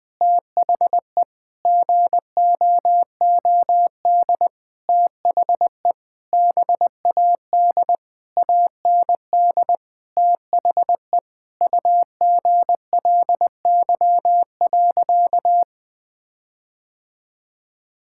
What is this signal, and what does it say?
Morse code, heard as a keyed tone, about 20 wpm, 700 Hz. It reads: THE GOOD THE BAD AND THE UGLY.
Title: Sentences with 7-Word Phrases - Spoken Beginning - 20wpm